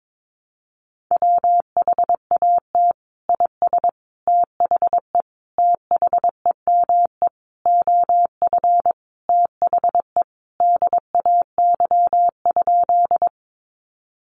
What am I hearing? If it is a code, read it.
Morse code, heard as a keyed tone, about 22 wpm, 700 Hz. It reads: WHAT IS THE THEME OF THE DAY?
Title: Sentences from Top 1100 Words 22wpm